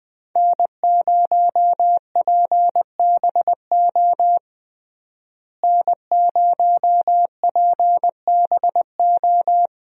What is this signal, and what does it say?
Morse code, heard as a keyed tone, about 20 wpm, 700 Hz. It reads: N0PBO N0PBO
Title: USA Call Signs - 5 Characters - Repeated 2x - 20wpm